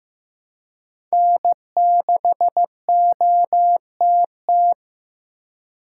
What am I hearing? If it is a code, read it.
Morse code, heard as a keyed tone, about 15 wpm, 700 Hz. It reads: N6OTT
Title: USA Call Signs - 4x Character Spacing - 15wpm